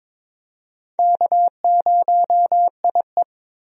Morse code, heard as a keyed tone, about 22 wpm, 700 Hz. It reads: K0IE
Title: USA Call Signs - 4 Characters - 22wpm